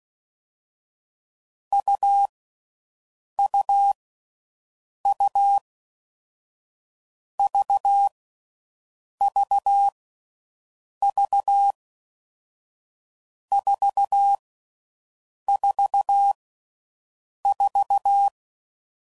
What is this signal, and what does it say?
Morse code, heard as a keyed tone, about 16 wpm, 800 Hz. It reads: U U U V V V 4 4 4
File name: Mixed / Short Runs / AUV AUV4